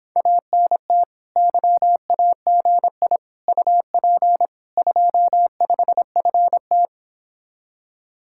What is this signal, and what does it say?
Morse code, heard as a keyed tone, about 26 wpm, 700 Hz. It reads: ANT YAGI UP 25FT